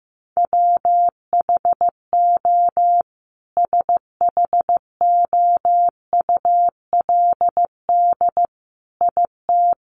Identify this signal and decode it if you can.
Morse code, heard as a keyed tone, about 15 wpm, 700 Hz. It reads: WHO SHOULD IT
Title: Sentences from Top 300 Words 15wpm